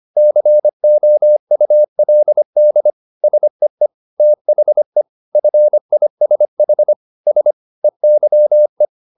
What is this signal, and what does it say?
Morse code, heard as a keyed tone, about 25 wpm, 600 Hz. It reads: COULD SEE THE FISH S EYE